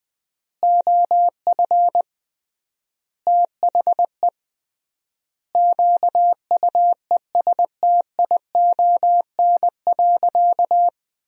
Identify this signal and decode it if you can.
Morse code, heard as a keyed tone, about 21 wpm, 700 Hz. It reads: OF THE QUESTION.
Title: 3x Word-Spacing - Sentences from Top 400 Words - 20wpm